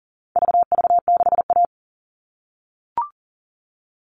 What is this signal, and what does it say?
Morse code, heard as a keyed tone, about 40 wpm, 700 Hz. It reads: VV6A
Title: All Cut Numbers - 4-Digit 40wpm